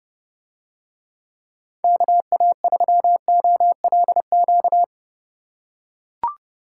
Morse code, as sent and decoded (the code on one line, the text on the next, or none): -.- .- ...-- --- .-.. --.-
KA3OLQ